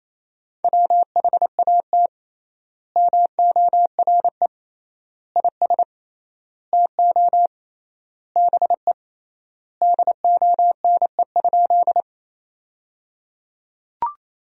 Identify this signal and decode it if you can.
Morse code, heard as a keyed tone, about 28 wpm, 700 Hz. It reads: WHAT MORE IS TO BE DONE?
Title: 3x Word-Spacing - Sentences from Top 500 Words - 28wpm